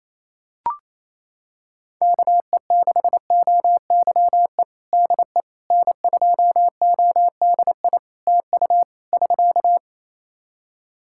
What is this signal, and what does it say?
Morse code, heard as a keyed tone, about 27 wpm, 700 Hz. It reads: KE6OYE DE N2ODI TU <SK>
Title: K1USN SST QSO Practice 28wpm